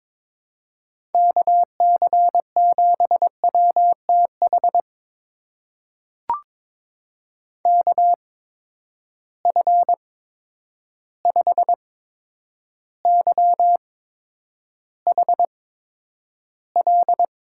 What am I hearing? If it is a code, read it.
Morse code, heard as a keyed tone, about 22 wpm, spaced as for 9 wpm, 700 Hz. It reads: KC7WTH KF5YHL
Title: USA Call Signs - 6 Characters - 8x Character Spacing - 22wpm